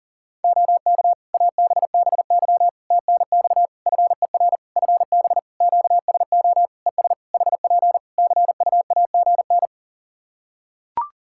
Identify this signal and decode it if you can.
Morse code, heard as a keyed tone, about 40 wpm, 700 Hz. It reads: OK ABBY TNX FER FB QSO ES HP CUAGN
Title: Practice QSO - Exchange 4 - Ending - 40wpm